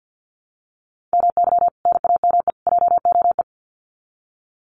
Morse code, heard as a keyed tone, about 50 wpm, 700 Hz. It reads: MY NAME JOE
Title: Common QSO Phrases 50wpm